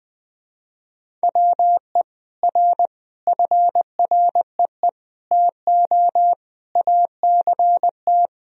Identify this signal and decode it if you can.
Morse code, heard as a keyed tone, about 21 wpm, 700 Hz. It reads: WE R FREE TO ACT
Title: Sentences from Top 500 Words and Abbreviations 20wpm